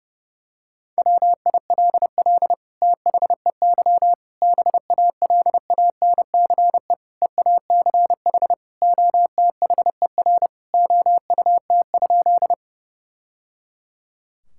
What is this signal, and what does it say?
Morse code, heard as a keyed tone, about 32 wpm, 700 Hz. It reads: WILL THEY BALANCE EACH OTHER OUT?
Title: Sentences from Top 1000 Words - Review Part 6 - 30wpm